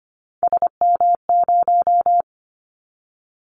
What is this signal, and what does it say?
Morse code, heard as a keyed tone, about 25 wpm, 700 Hz. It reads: SM0
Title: Callsign Prefixes for Top 25 DXCC - 25wpm